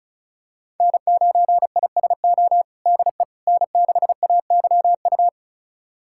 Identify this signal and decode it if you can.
Morse code, heard as a keyed tone, about 34 wpm, 700 Hz. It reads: N9ISO DE N6AYU
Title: USA Callsign DE Callsign - 5 Characters - 35wpm